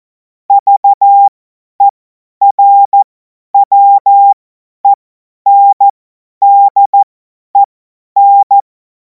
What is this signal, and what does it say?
Morse code, heard as a keyed tone, about 14 wpm, spaced as for 10 wpm, 800 Hz. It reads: VERWENDEN
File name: Morsekurs / Witze / Witz 23. Juni 2026 50 F70